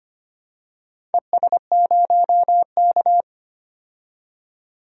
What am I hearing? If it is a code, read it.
Morse code, heard as a keyed tone, about 25 wpm, 700 Hz. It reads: ES0K
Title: DXCC-100 Call Signs - 4 Characters - 25wpm